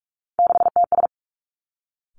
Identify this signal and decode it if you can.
Morse code, heard as a keyed tone, about 46 wpm, 700 Hz. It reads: 6TS
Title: Three Challenging Letter-Number Combinations 45wpm